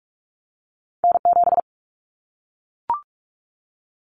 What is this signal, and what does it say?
Morse code, heard as a keyed tone, about 44 wpm, 700 Hz. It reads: N7
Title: Callsign Prefixes for Top 25 DXCC - Repeated 3x - 45wpm